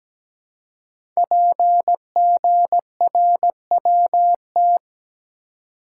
Morse code, heard as a keyed tone, about 17 wpm, 700 Hz. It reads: PGRWT
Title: Code Groups - 5 Letters - 17wpm